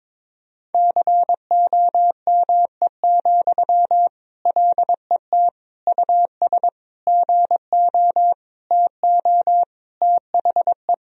Morse code, heard as a keyed tone, about 21 wpm, 700 Hz. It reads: COME, LET US GO TO THE
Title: Sentences from Top 1000 Words - Review Part 4 - 22wpm